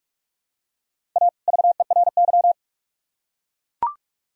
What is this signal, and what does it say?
Morse code, heard as a keyed tone, about 44 wpm, 700 Hz. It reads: A VERY